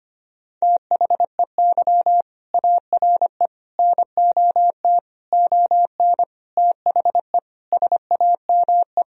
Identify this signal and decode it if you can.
Morse code, heard as a keyed tone, about 25 wpm, 700 Hz. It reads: THEY ARE NOT ON THE SAME
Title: Sentences from Top 1000 Words - Review Part 4 - 25wpm